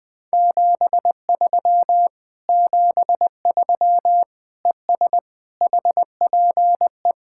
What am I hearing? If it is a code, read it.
Morse code, heard as a keyed tone, about 20 wpm, 700 Hz. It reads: 73 73 ES HPE
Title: Speed-Racing Common QSO Phrases - 20wpm